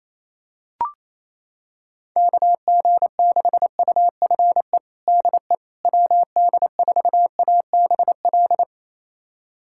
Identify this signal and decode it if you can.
Morse code, heard as a keyed tone, about 28 wpm, 700 Hz. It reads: KG6UFE DE WD4ABL
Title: USA Callsign DE Callsign 28wpm